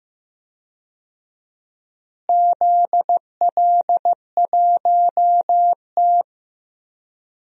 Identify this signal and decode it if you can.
Morse code, heard as a keyed tone, about 15 wpm, 700 Hz. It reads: ZL1T
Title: DXCC-100 Call Signs - 4 Characters - Repeated 2x - 15wpm